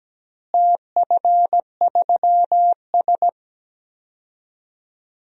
Morse code, heard as a keyed tone, about 17 wpm, 700 Hz. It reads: TF3S
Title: DXCC-100 Call Signs - 4 Characters - Repeated 2x - 17wpm